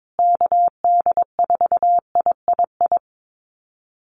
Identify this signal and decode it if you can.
Morse code, heard as a keyed tone, about 22 wpm, 700 Hz. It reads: KD4III
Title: USA Call Signs Repeated 2x 22wpm